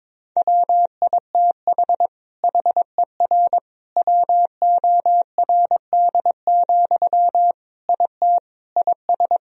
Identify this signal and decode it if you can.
Morse code, heard as a keyed tone, about 22 wpm, 700 Hz. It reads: WITH HER WORD, IT IS
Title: Sentences from Top 1000 Words - Review Part 2 - 22wpm